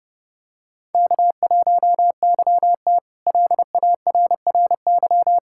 Morse code, heard as a keyed tone, about 32 wpm, 700 Hz. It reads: K1YT LARRY
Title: K1USN SST QSO Practice - 30wpm